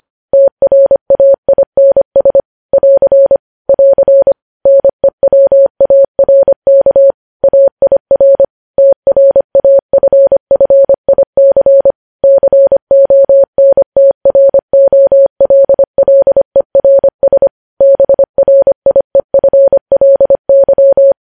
Morse code, heard as a keyed tone, about 25 wpm, 550 Hz. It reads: TRAINS <AR> <AR> NEWARK AIR TRAFFIC CONTROLLERS BRIEFLY
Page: News Headlines in Morse Code at 25 WPM